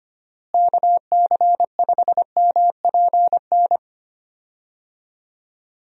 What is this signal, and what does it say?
Morse code, heard as a keyed tone, about 25 wpm, 700 Hz. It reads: KC5MPN
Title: USA Call Signs - 6 Characters - Repeated 2x - 25wpm